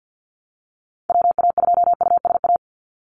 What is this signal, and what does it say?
Morse code, heard as a keyed tone, about 49 wpm, 700 Hz. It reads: WA2USA
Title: NA SOTA Chaser Callsigns 50wpm